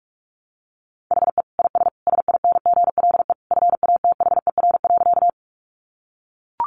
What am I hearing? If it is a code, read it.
Morse code, heard as a keyed tone, about 44 wpm, 700 Hz. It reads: HE IS SINGLE FATHER.
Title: Sentences from Top 700 Words 45wpm